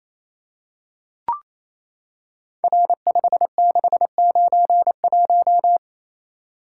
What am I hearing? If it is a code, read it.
Morse code, heard as a keyed tone, about 27 wpm, 700 Hz. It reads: R5691
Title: Code Groups - 5 Letter:Numbers - 28wpm